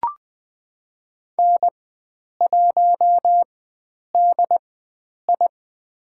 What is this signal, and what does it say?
Morse code, heard as a keyed tone, about 20 wpm, spaced as for 9 wpm, 700 Hz. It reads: N1DI
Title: USA Call Signs - 4 Characters - 4x Character Spacing - 20wpm